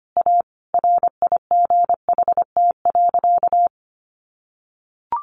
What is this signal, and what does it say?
Morse code, heard as a keyed tone, about 25 wpm, 700 Hz. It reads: A RIGHT.
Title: Sentences from Top 1400 Words 25wpm